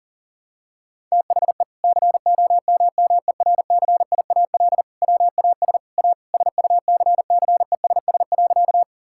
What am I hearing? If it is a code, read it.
Morse code, heard as a keyed tone, about 40 wpm, 700 Hz. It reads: THE COMMERCIAL WAS A SUCCESS.
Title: Sentences from Top 1000 Words - Review Part 5 - 40wpm